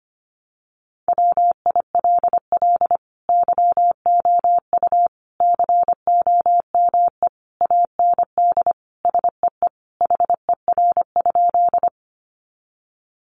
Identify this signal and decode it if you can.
Morse code, heard as a keyed tone, about 25 wpm, 700 Hz. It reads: WILL YOU COME AND SEE HER?
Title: Sentences from Top 100 Words 25wpm